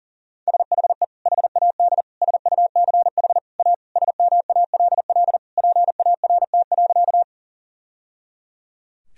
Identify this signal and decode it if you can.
Morse code, heard as a keyed tone, about 37 wpm, 700 Hz. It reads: SHE HAD SUCH A SMALL PART.
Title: Sentences from Top 1000 Words - Review Part 1 - 40wpm